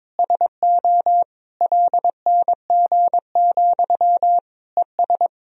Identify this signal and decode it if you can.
Morse code, heard as a keyed tone, about 22 wpm, 700 Hz. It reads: SO LNG, ES
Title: Sentences from Top 500 Words and Abbreviations 22wpm